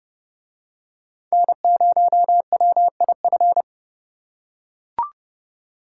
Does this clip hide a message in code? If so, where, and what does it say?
Yes: Morse code from 1.3 to 3.6 s, heard as a keyed tone, about 30 wpm, 700 Hz. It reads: N0WIF